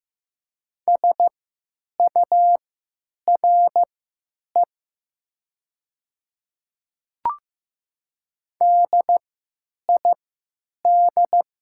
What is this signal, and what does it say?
Morse code, heard as a keyed tone, about 15 wpm, spaced as for 9 wpm, 700 Hz. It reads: SURE DID
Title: 3x Character Spacing - Top 300 Words - 15wpm